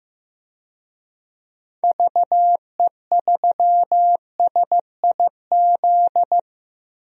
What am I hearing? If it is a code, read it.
Morse code, heard as a keyed tone, about 15 wpm, 700 Hz. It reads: VE3SIZ
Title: DXCC-100 Call Signs - 6 Characters - 15wpm